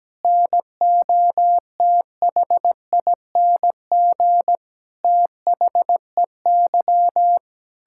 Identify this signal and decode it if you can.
Morse code, heard as a keyed tone, about 17 wpm, 700 Hz. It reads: NOTHING THEY
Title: Sentences from Top 1000 Words - Review Part 6 - 17wpm